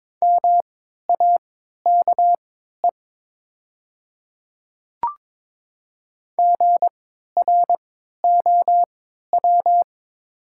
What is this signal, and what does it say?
Morse code, heard as a keyed tone, about 22 wpm, spaced as for 12 wpm, 700 Hz. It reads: MAKE GROW
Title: 3x Character Spacing - Top 500 Words - 22wpm